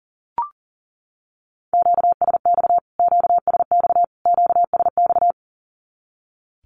Morse code, heard as a keyed tone, about 40 wpm, 700 Hz. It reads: QSX QSX QSX
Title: Q-Codes Repeated 3x 40wpm